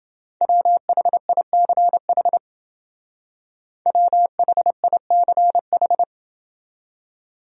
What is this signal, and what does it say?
Morse code, heard as a keyed tone, about 29 wpm, 700 Hz. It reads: WHICH WHICH